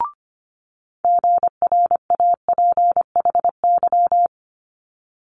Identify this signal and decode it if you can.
Morse code, heard as a keyed tone, about 25 wpm, 700 Hz. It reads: GRAPHY
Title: Multi-letter Phonemes Quick Fire 25wpm